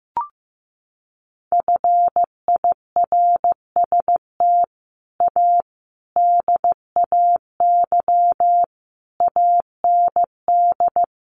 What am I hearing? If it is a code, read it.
Morse code, heard as a keyed tone, about 15 wpm, 700 Hz. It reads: FIRST A DAY AND
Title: A Sound Of Thunder 15wpm